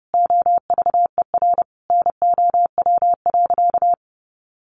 Morse code, heard as a keyed tone, about 30 wpm, 700 Hz. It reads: OVER NOW.
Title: Sentences from Top 1300 Words 30wpm